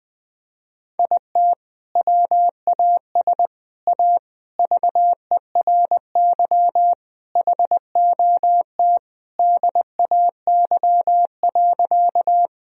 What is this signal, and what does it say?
Morse code, heard as a keyed tone, about 20 wpm, 700 Hz. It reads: IT WAS A VERY HOT DAY.
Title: Sentences from Top 1000 Words - Review Part 3 - 20wpm